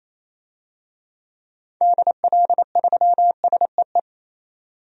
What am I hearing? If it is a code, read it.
Morse code, heard as a keyed tone, about 28 wpm, 700 Hz. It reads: DL3SEE
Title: DXCC-100 Call Signs - 6 Characters - 28wpm